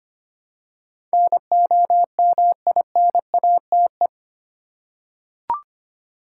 Morse code, heard as a keyed tone, about 25 wpm, 700 Hz. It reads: NOMINATE
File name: 8 Letter Words - First 3 Letters Given - 25wpm